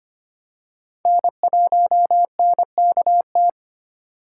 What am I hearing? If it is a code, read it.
Morse code, heard as a keyed tone, about 25 wpm, 700 Hz. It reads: N1NKT
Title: USA Call Signs - 5 Characters - 4x Character Spacing - 25wpm